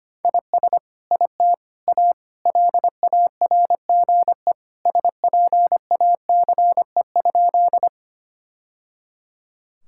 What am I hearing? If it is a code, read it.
Morse code, heard as a keyed tone, about 25 wpm, 700 Hz. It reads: IS IT A LARGE SPACE?
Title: Sentences from Top 1000 Words - Review Part 5 - 25wpm